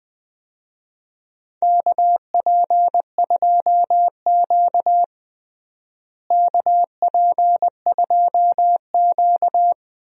Morse code, heard as a keyed tone, about 20 wpm, 700 Hz. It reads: KP2Q KP2Q
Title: DXCC-100 Call Signs - 4 Characters - Repeated 2x - 20wpm